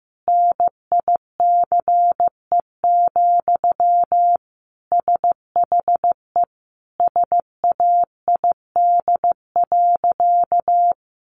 Morse code, heard as a keyed tone, about 15 wpm, 700 Hz. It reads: NICE, SHE SAID.
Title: Sentences from Top 1100 Words 15wpm